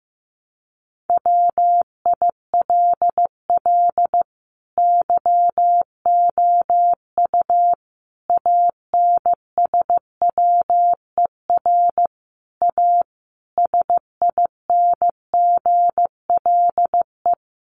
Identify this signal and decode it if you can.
Morse code, heard as a keyed tone, about 15 wpm, 700 Hz. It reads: WILL YOU ANSWER A SINGLE
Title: Sentences from Top 700 Words 15wpm